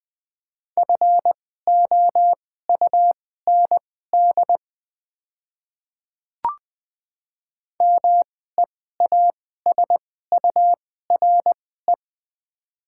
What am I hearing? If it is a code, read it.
Morse code, heard as a keyed tone, about 20 wpm, spaced as for 14 wpm, 690 Hz. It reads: FOUND MEASURE
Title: 2x Character Spacing - Top 500 Words - 20wpm